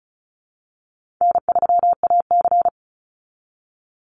Morse code, heard as a keyed tone, about 34 wpm, 700 Hz. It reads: N3AC
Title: Easy Call Signs Repeated 3x - 35wpm